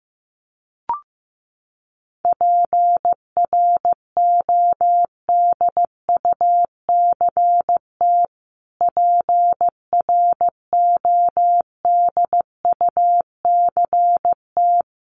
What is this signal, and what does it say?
Morse code, heard as a keyed tone, about 14 wpm, 700 Hz. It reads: PRODUCT PRODUCT
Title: Top 500 Words Repeated 15wpm